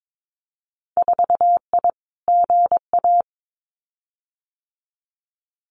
Morse code, heard as a keyed tone, about 22 wpm, 700 Hz. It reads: 4I GA
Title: Winter Field Day Exchanges Encore 22wpm